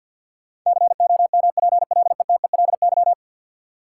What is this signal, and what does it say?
Morse code, heard as a keyed tone, about 50 wpm, 700 Hz. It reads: COMPLETELY
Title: Sentences from Top 1100 Words - New Word Drill - 50wpm